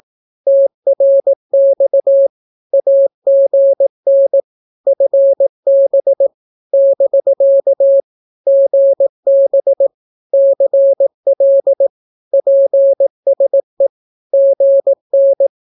Morse code, heard as a keyed tone, about 18 wpm, 550 Hz. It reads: TRX AGN FB <BK> GB CL PSE GN
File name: naukapodstaw / skroty / skroty